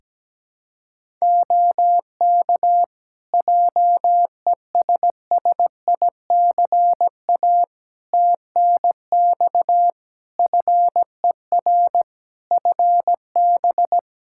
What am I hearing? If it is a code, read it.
Morse code, heard as a keyed tone, about 17 wpm, 700 Hz. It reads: OK JESSICA TNX FER FB
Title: Practice QSO - Exchange 4 - Ending - 17wpm